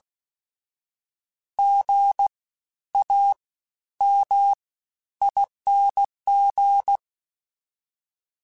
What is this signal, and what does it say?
Morse code, heard as a keyed tone, about 16 wpm, 800 Hz. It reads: G A M ING
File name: Special INGs-mp3s / Special INGs-mp3s / Fast ing @15